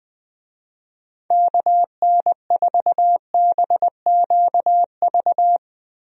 Morse code, heard as a keyed tone, about 20 wpm, 700 Hz. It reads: KN4BQV
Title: USA Call Signs - 6 Characters - 20wpm